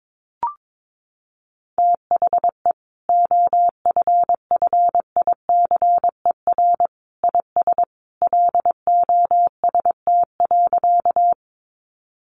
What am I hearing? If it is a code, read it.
Morse code, heard as a keyed tone, about 22 wpm, 700 Hz. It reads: THE OFFICER IS LOST.
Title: Sentences from Top 1400 Words 22wpm